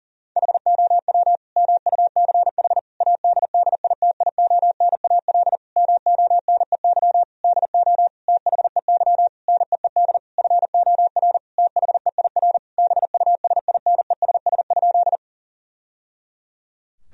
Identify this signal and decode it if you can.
Morse code, heard as a keyed tone, about 40 wpm, 700 Hz. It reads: HOW MUCH ADDITIONAL MONEY DO THEY NEED FOR THEIR BUSINESS?
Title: Sentences from Top 1000 Words - Review Part 4 - 40wpm